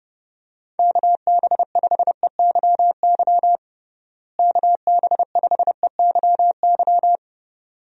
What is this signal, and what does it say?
Morse code, heard as a keyed tone, about 32 wpm, 700 Hz. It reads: KB5EYY KB5EYY
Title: USA Call Signs - 6 Characters - Repeated 2x - 30wpm